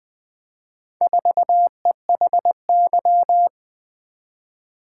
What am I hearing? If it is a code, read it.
Morse code, heard as a keyed tone, about 20 wpm, 700 Hz. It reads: 4EHY
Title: Four Letter-Number Combinations - 20wpm